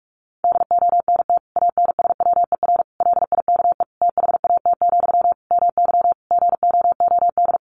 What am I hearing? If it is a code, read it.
Morse code, heard as a keyed tone, about 44 wpm, 700 Hz. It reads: DONT ANSWER LIKE THAT, MY GOOD
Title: Sentences from Top 400 Words 45wpm